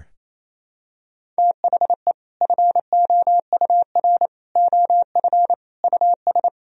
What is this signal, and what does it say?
Morse code, heard as a keyed tone, about 27 wpm, 700 Hz. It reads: THE FOUR OF US
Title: Sentences from Top 1000 Words - Review Part 1 - 28wpm